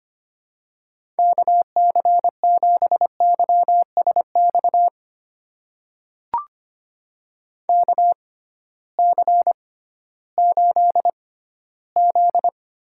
Morse code, heard as a keyed tone, about 25 wpm, spaced as for 14 wpm, 700 Hz. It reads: KC7YSX KC8Z
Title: USA Call Signs - 6 Characters - 6x Character Spacing - 25wpm